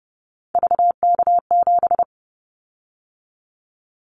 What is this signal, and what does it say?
Morse code, heard as a keyed tone, about 30 wpm, 700 Hz. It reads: VK7
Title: Callsign Prefixes for Top 100 DXCC - 30wpm